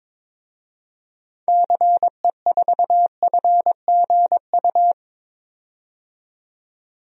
Morse code, heard as a keyed tone, about 22 wpm, 700 Hz. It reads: CE4FGU
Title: DXCC-100 Call Signs - 6 Characters - 22wpm